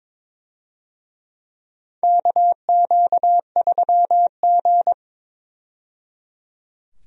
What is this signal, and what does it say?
Morse code, heard as a keyed tone, about 22 wpm, 700 Hz. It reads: KQ3G
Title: USA Call Signs - 4 Characters - 22wpm